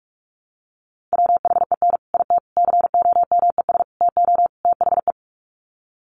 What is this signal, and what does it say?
Morse code, heard as a keyed tone, about 44 wpm, 700 Hz. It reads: WHEN IT COMES TO THE
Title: Sets of 5 Words 45wpm